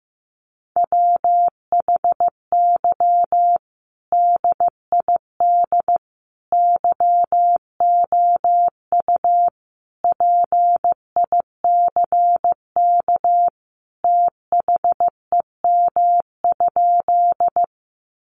Morse code, heard as a keyed tone, about 15 wpm, 700 Hz. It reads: WHY DID YOU PICK THEM?
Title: Sentences from Top 1000 Words 15wpm